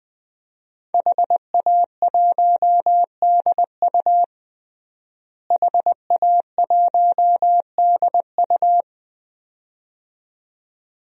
Morse code, heard as a keyed tone, about 21 wpm, 700 Hz. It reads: HA1DU HA1DU
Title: DXCC-100 Call Signs - 5 Characters - Repeated 2x - 20wpm